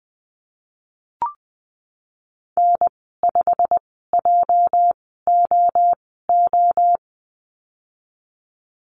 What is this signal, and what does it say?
Morse code, heard as a keyed tone, about 20 wpm, spaced as for 15 wpm, 700 Hz. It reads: N5JOO
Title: 207 Lesson 33g - Letter j - Call Signs 20wpm